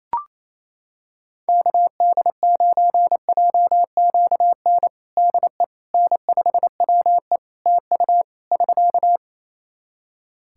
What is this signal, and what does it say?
Morse code, heard as a keyed tone, about 27 wpm, 700 Hz. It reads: KD9JQN DE N5WE TU <SK>
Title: Practice QSO - Exchange 4 - Ending - 28wpm